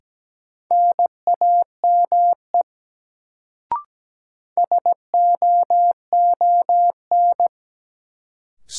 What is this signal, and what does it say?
Morse code, heard as a keyed tone, about 17 wpm, 700 Hz. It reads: NAME SOON
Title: Sets of 3 Words - New Word Drill - 17wpm